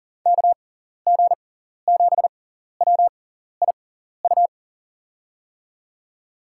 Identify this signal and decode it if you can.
Morse code, heard as a keyed tone, about 40 wpm, spaced as for 14 wpm, 700 Hz. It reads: KG7WIU
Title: USA Call Signs - 6 Characters - 6x Character Spacing - 40wpm